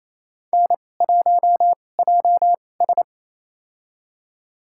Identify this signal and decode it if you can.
Morse code, heard as a keyed tone, about 28 wpm, spaced as for 20 wpm, 700 Hz. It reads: N1JS
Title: USA Call Signs - 4 Characters - 2x Character Spacing - 28wpm